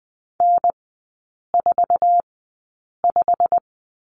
Morse code, heard as a keyed tone, about 20 wpm, spaced as for 8 wpm, 700 Hz. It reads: N45
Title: 127 Lesson 21e - Number 6 - Three Characters 20wpm